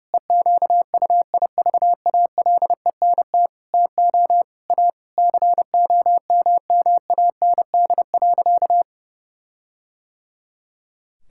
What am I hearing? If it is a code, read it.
Morse code, heard as a keyed tone, about 32 wpm, 700 Hz. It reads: EQUIVALENT TO A COMMAND.
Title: Sentences from Top 1600 Words 30wpm